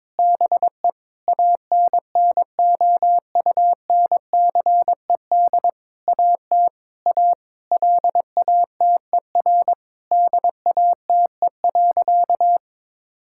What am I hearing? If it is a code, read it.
Morse code, heard as a keyed tone, about 22 wpm, 700 Hz. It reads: BE ANNOUNCED AT A LATER DATE.
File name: Sentences with 7-Word Phrases - 22wpm